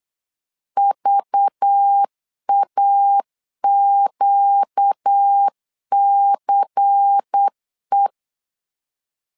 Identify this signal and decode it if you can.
Morse code, heard as a keyed tone, about 8 wpm, 800 Hz. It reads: VAQCE
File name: LEZIONI / 5 LBVC / 5 LBVC 8wpm 2935